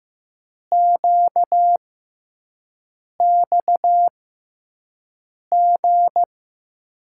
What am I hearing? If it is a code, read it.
Morse code, heard as a keyed tone, about 15 wpm, spaced as for 5 wpm, 700 Hz. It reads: QXG
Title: USA Call Signs - 5 Characters - 6x Character Spacing - 15wpm